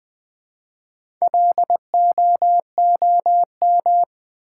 Morse code, heard as a keyed tone, about 20 wpm, 700 Hz. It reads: LOOM
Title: Build A Word - 1 Letter At A Time - 4-Letter Words - 20wpm